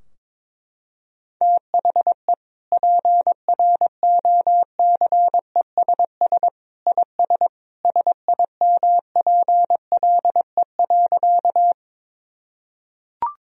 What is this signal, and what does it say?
Morse code, heard as a keyed tone, about 22 wpm, 700 Hz. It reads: THE PROCESS IS SIMPLE.
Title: Sentences from Top 1000 Words - Review Part 3 - 22wpm